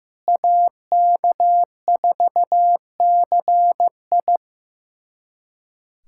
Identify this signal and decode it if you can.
Morse code, heard as a keyed tone, about 15 wpm, 700 Hz. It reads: AK4CI
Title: USA Callsign DE Callsign - 5 Characters - 15wpm